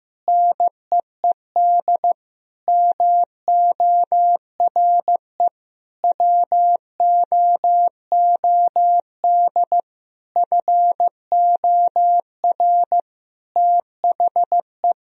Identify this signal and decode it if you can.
Morse code, heard as a keyed tone, about 15 wpm, 700 Hz. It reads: NEED MORE WOOD FOR THE
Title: Sentences from Top 1000 Words - Review Part 2 - 15wpm